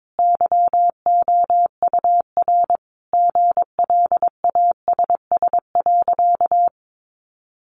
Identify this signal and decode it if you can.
Morse code, heard as a keyed tone, about 22 wpm, 700 Hz. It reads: YOUR GLASS.
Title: Sentences from Top 1300 Words 22wpm